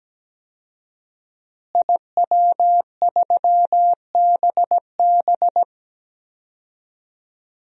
Morse code, heard as a keyed tone, about 17 wpm, 700 Hz. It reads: IW3BB
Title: DXCC-100 Call Signs - 5 Characters - Repeated 3x - 17wpm